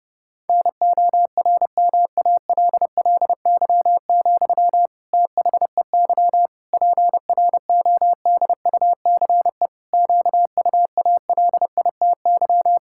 Morse code, heard as a keyed tone, about 32 wpm, 700 Hz. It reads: NORMALLY, THEY PRODUCE QUALITY
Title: Sentences from Top 1000 Words - Review Part 6 - 30wpm